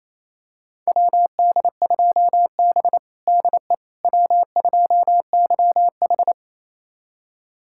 Morse code, heard as a keyed tone, about 28 wpm, 690 Hz. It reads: WD2B DE W2YH
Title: USA Callsign DE Callsign - 4 Characters - 28wpm